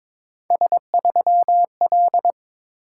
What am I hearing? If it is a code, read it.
Morse code, heard as a keyed tone, about 22 wpm, 700 Hz. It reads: S3L
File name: practice / 22wpm / Three Letter-Number Combinations - 22wpm